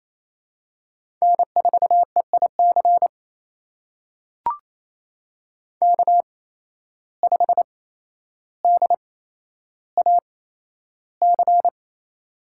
Morse code, heard as a keyed tone, about 28 wpm, spaced as for 12 wpm, 700 Hz. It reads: N4EIC K5DAC
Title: USA Call Signs - 5 Characters - 8x Character Spacing - 28wpm